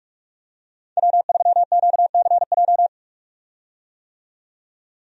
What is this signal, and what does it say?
Morse code, heard as a keyed tone, about 46 wpm, 700 Hz. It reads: W3QCJ
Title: USA Call Signs - 5 Characters - 45wpm